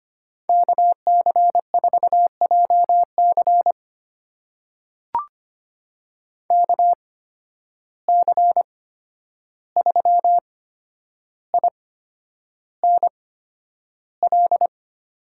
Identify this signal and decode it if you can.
Morse code, heard as a keyed tone, about 25 wpm, spaced as for 10 wpm, 700 Hz. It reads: KC4JC KC3INL
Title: USA Call Signs - 8x Character Spacing - 25wpm